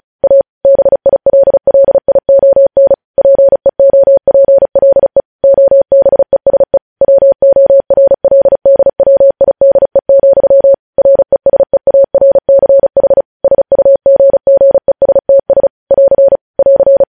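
Morse code, heard as a keyed tone, about 36 wpm, 550 Hz. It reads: A BILLION PEOPLE OBESE WORLDWIDE, RESEARCH SUGGESTS <AR> <AR>
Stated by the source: Morse code practice sending news headlines at 35 wpm.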